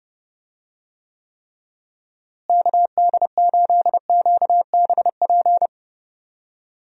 Morse code, heard as a keyed tone, about 30 wpm, 700 Hz. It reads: KD8QBP
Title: USA Call Signs - 6 Characters - Repeated 3x - 30wpm